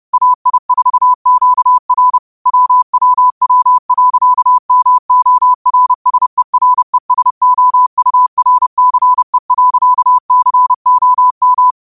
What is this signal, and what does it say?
Morse code, heard as a keyed tone, about 30 wpm, 1000 Hz. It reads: AI4QR WWW.MORSERESOURCE.COM